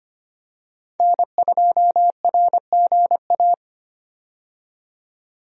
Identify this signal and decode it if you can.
Morse code, heard as a keyed tone, about 25 wpm, 700 Hz. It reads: N2RGA
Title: USA Call Signs - 5 Characters - 25wpm